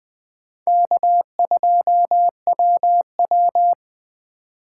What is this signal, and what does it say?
Morse code, heard as a keyed tone, about 20 wpm, 700 Hz. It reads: K2WW
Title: USA Call Signs - 4x Character Spacing - 20wpm